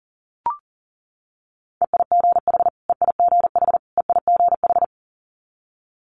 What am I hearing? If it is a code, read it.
Morse code, heard as a keyed tone, about 39 wpm, 700 Hz. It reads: EIGH EIGH EIGH
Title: Multi-letter Phonemes Repeated 3x 40wpm